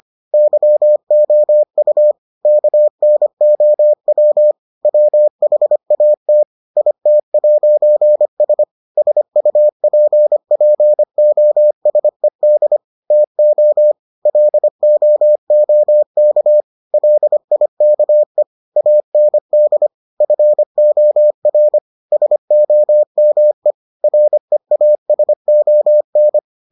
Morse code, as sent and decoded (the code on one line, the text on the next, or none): -.-- --- ..- / -.- -. --- .-- / .-- .... .- - / .. - .----. ... / ... ..- .--. .--. --- ... -.. / - --- / .-.. --- --- -.- / .-.. .. -.- . / .- -. -.. / ..-. --- .-. / ... --- -- . / .-. .- ... --- -.
YOU KNOW WHAT IT'S SUPPOSED TO LOOK LIKE AND FOR SOME REASON